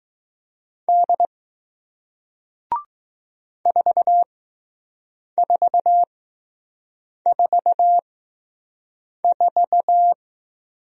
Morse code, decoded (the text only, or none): D 4444